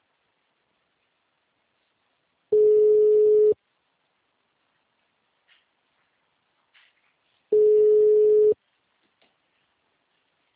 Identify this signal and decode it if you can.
Ringback tone (Europe, 425 Hz).